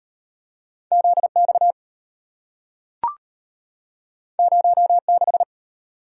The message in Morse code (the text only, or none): ZX 06